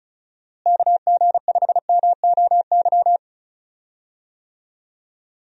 Morse code, heard as a keyed tone, about 34 wpm, 700 Hz. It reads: KG5MOY
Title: USA Call Signs - 6 Characters - Repeated 2x - 35wpm